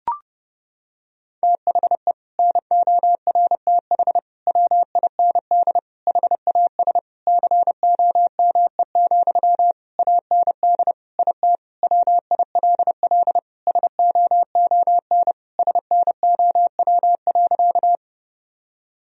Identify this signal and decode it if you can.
Morse code, heard as a keyed tone, about 32 wpm, 700 Hz. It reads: THE NORTH WIND HAS COME, AND IT WILL SOON SNOW.
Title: Sentences from Top 1000 Words - Review Part 3 - 30wpm